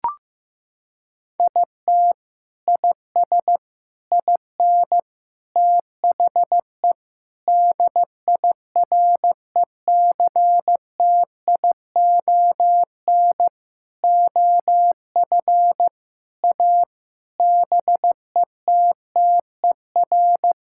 Morse code, heard as a keyed tone, about 15 wpm, 700 Hz. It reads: IT IS IN THE DIRECTION OF A BETTER
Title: Sentences from Top 1000 Words - Review Part 5 - 15wpm